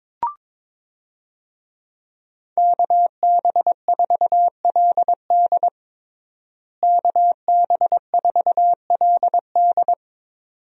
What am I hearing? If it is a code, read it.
Morse code, heard as a keyed tone, about 22 wpm, 700 Hz. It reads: KB4LD KB4LD
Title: USA Call Signs - 5 Characters - Repeated 2x - 22wpm